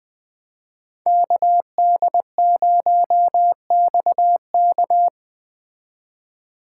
Morse code, heard as a keyed tone, about 20 wpm, 700 Hz. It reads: KD0XK
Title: Speed Racing - USA 2x2 Call Signs 17wpm